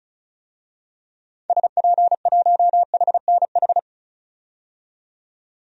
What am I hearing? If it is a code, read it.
Morse code, heard as a keyed tone, about 34 wpm, 700 Hz. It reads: SP1HNH
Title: DXCC-100 Call Signs - 6 Characters - 35wpm